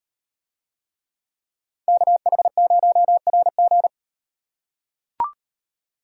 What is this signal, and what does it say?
Morse code, heard as a keyed tone, about 38 wpm, 700 Hz. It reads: KH0RG